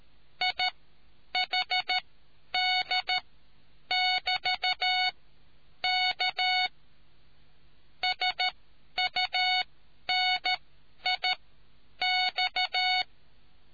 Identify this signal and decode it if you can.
Morse code, heard as a keyed tone, about 13 wpm, spaced as for 9 wpm, 740 Hz. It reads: IHD<BT>K SUNIX